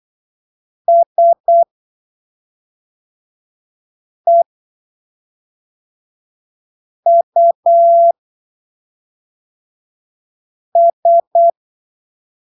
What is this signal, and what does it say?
Morse code, heard as a keyed tone, about 8 wpm, spaced as for 3 wpm, 680 Hz. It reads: SEUS
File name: arquivos / texto / texto 06 08ppm